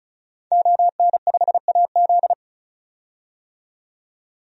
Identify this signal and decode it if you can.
Morse code, heard as a keyed tone, about 34 wpm, 700 Hz. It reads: ON5AZ
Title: DXCC-100 Call Signs - 5 Characters - 35wpm